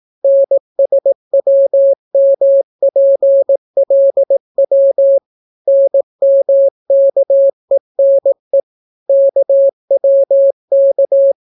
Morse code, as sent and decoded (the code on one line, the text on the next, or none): -. ... .-- -- .--. .-.. .-- / -. -- -.- . -. / -.- .-- -.-
NSWMPLW NMKENE KWK